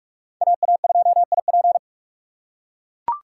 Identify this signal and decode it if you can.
Morse code, heard as a keyed tone, about 41 wpm, 690 Hz. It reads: AA2IP
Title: USA Call Signs - 5 Characters - 6x Character Spacing - 45wpm